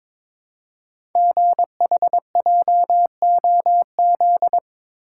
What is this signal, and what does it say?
Morse code, heard as a keyed tone, about 22 wpm, 700 Hz. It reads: GHJOZ
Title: Code Groups - 5 Letters - 22wpm